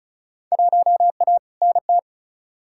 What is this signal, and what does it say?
Morse code, heard as a keyed tone, about 36 wpm, 700 Hz. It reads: 1A NT
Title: Field Day Exchanges Encore - 35wpm